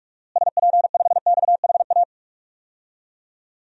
Morse code, heard as a keyed tone, about 46 wpm, 700 Hz. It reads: SP5XHA
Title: DXCC-100 Call Signs - 6 Characters - Repeated 2x - 45wpm